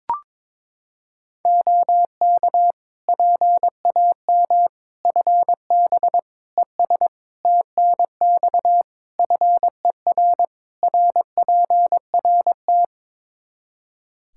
Morse code, heard as a keyed tone, about 22 wpm, 700 Hz. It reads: OK PAM FB ES TNX FER RPRT
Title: K1USN SST QSO Practice 22wpm